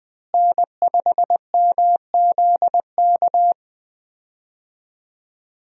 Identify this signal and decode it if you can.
Morse code, heard as a keyed tone, about 20 wpm, 700 Hz. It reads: N5MZK
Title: USA Call Signs - 5 Characters - Repeated 2x - 20wpm